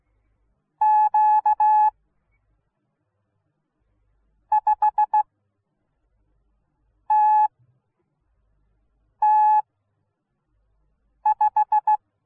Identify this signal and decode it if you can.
Morse code, heard as a keyed tone, about 16 wpm, spaced as for 4 wpm, 850 Hz. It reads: Q5TT5